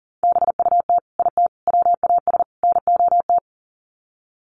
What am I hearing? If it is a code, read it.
Morse code, heard as a keyed tone, about 40 wpm, 700 Hz. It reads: BUT IT WAS NOT
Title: Sets of 4 Words 40wpm